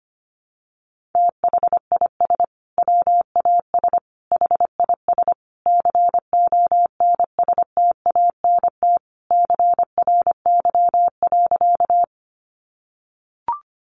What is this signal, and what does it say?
Morse code, heard as a keyed tone, about 25 wpm, 700 Hz. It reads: THIS WAS HIS CONSTANT CRY.
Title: Sentences from Top 1500 Words 25wpm